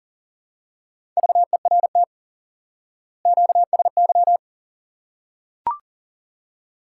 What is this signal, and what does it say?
Morse code, heard as a keyed tone, about 39 wpm, 700 Hz. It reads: VERT QSY
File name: IQE Course 3 - Lesson 89 - YAGI - Two Elements - 40wpm